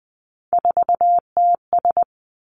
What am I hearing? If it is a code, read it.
Morse code, heard as a keyed tone, about 20 wpm, 700 Hz. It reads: 4TS
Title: Three Challenging Letter-Number Combinations 20wpm